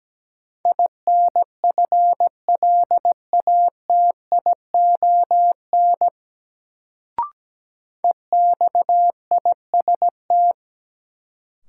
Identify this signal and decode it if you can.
Morse code, heard as a keyed tone, about 17 wpm, 700 Hz. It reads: INFLATION EXIST
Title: Sentences from Top 1100 Words - New Word Drill Alt - 17wpm